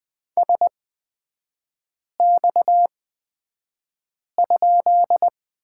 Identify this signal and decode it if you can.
Morse code, heard as a keyed tone, about 20 wpm, spaced as for 5 wpm, 700 Hz. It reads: SX?
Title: 243 Lesson 39e - Letter z - Three Characters 20wpm